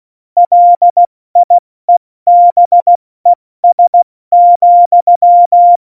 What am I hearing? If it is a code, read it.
Morse code, heard as a keyed tone, about 16 wpm, 700 Hz. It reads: LIEBES,
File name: Morsekurs / Witze / Witz 27. August 2019, 70 F80